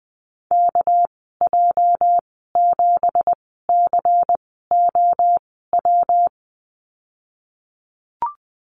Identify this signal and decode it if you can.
Morse code, heard as a keyed tone, about 20 wpm, spaced as for 15 wpm, 700 Hz. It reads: KJ7COW
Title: 213 Lesson 34g - Number 8 - Call Signs 20wpm